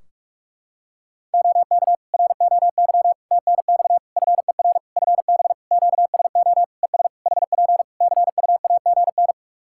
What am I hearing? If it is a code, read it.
Morse code, heard as a keyed tone, about 46 wpm, 700 Hz. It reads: OK ROY TNX FER FB QSO ES HP CUAGN
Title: Practice QSO - Exchange 4 - Ending - 45wpm